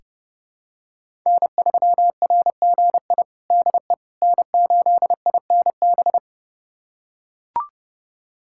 Morse code, heard as a keyed tone, about 31 wpm, 700 Hz. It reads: N3RGI DE N8INB
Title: USA Callsign DE Callsign - 5 Characters - 30wpm